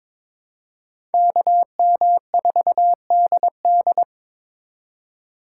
Morse code, heard as a keyed tone, about 26 wpm, 700 Hz. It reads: KM4DD
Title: Speed Racing - USA Call Signs 22wpm